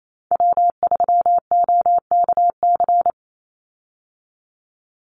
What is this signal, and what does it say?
Morse code, heard as a keyed tone, about 28 wpm, 690 Hz. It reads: W3OKC
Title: Call Signs Repeated 3x - 28wpm